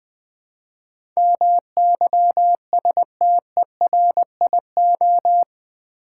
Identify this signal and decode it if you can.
Morse code, heard as a keyed tone, about 20 wpm, 700 Hz. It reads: MYSTERIO
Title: Build A Word - 4 Letters At A Time - 10-Letter Words - 20wpm